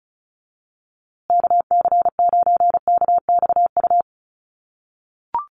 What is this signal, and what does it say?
Morse code, heard as a keyed tone, about 34 wpm, 700 Hz. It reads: KC9KXU
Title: USA Call Signs Repeated 3x 35wpm